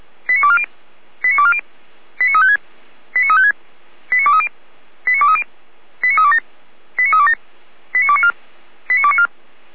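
Five-tone selective call (ZVEI), heard as quick runs of five tones, tones of about 70 ms. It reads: 89240 89240 89356 89356 89230 89230 89327 89327 89294 89294
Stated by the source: ZVEI five-tone selective call received via SDR.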